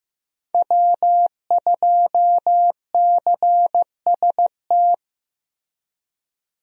Morse code, heard as a keyed tone, about 15 wpm, 700 Hz. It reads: W2CST
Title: USA Call Signs - 5 Characters - Repeated 2x - 15wpm